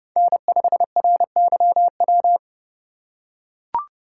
Morse code, heard as a keyed tone, about 30 wpm, 700 Hz. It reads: N5RYW
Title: USA Call Signs - 2x Character Spacing - 30wpm